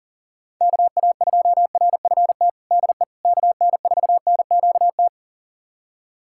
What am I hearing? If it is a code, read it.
Morse code, heard as a keyed tone, about 40 wpm, 700 Hz. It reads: KA2RFT DE KN4NQT
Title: USA Callsign DE Callsign 40wpm